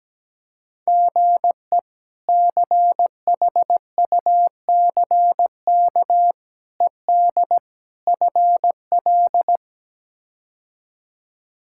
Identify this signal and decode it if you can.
Morse code, heard as a keyed tone, about 17 wpm, 700 Hz. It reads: GE CHUCK ED FL
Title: K1USN SST QSO Practice - 17wpm